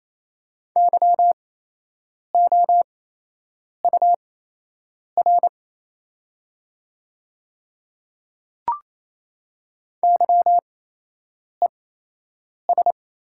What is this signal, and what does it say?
Morse code, heard as a keyed tone, about 28 wpm, spaced as for 8 wpm, 700 Hz. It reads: YOUR YES
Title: 8x Character Spacing - Top 500 Words - 28wpm